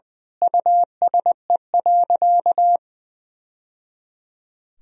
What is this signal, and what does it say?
Morse code, heard as a keyed tone, about 20 wpm, 700 Hz. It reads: USE.
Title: Sentences from Top 1000 Words - Review Part 3 - 20wpm